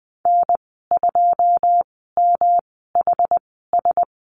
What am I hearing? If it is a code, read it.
Morse code, heard as a keyed tone, about 20 wpm, spaced as for 15 wpm, 700 Hz. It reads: N2MHS
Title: 122 Lesson 20g - Number 3 - Call Signs 20wpm